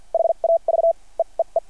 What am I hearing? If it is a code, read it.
Morse code, heard as a keyed tone, about 51 wpm, spaced as for 34 wpm, 660 Hz. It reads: HAVE EE
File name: archived / CRSnet